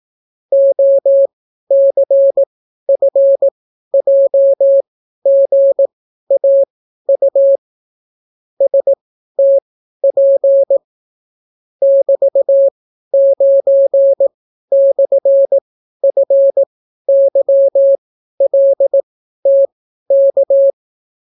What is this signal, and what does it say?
Morse code, heard as a keyed tone, about 18 wpm, spaced as for 12 wpm, 550 Hz. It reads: OCFJGAU STP <BT>9/FYLTK